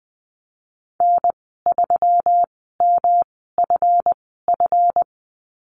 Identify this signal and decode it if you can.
Morse code, heard as a keyed tone, about 20 wpm, spaced as for 15 wpm, 700 Hz. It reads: N3MFF
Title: 179 Lesson 29g - Slash - Call Signs 20wpm